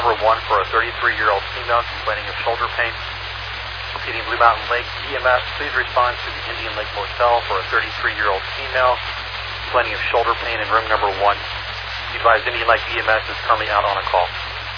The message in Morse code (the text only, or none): none